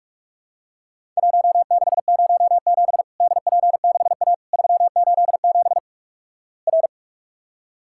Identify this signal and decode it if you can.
Morse code, heard as a keyed tone, about 46 wpm, 700 Hz. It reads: 1607 DP6A 387 R
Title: Worked All Europe DX Contest - QTC Exchange - 45wpm